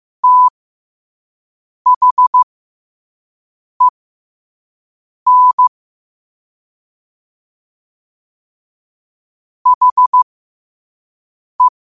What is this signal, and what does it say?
Morse code, heard as a keyed tone, about 15 wpm, spaced as for 5 wpm, 1000 Hz. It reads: THEN HE